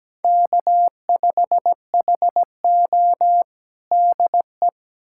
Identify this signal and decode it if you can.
Morse code, heard as a keyed tone, about 17 wpm, 700 Hz. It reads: K5HO DE
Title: USA Callsign DE Callsign - 4 Characters - 17wpm